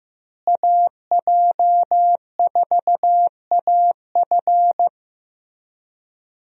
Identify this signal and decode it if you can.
Morse code, heard as a keyed tone, about 16 wpm, 700 Hz. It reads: AJ4AF
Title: USA Call Signs - 5 Characters - Repeated 3x - 15wpm